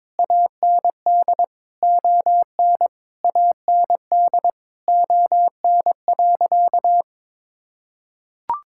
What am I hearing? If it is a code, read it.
Morse code, heard as a keyed tone, about 22 wpm, 700 Hz. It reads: AND ON AND ON.
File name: practice / 22wpm / Sentences with 8-Word Phrases - 22wpm